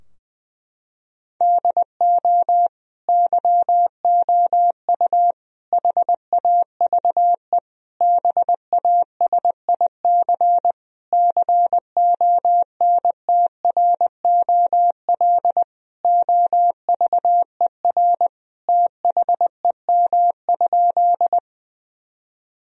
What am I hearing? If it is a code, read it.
Morse code, heard as a keyed tone, about 20 wpm, 700 Hz. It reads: DO YOU HAVE BASIC CONTROL OVER THEM?
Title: Sentences from Top 1000 Words - Review Part 3 - 20wpm